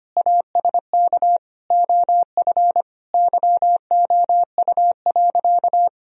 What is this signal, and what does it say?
Morse code, heard as a keyed tone, about 25 wpm, 700 Hz. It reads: ASK OF YOU.
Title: Sentences from Top 1000 Words - Review Part 4 - 25wpm